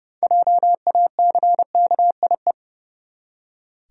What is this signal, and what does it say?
Morse code, heard as a keyed tone, about 32 wpm, 700 Hz. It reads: JACKIE
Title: USA Common Names - 30wpm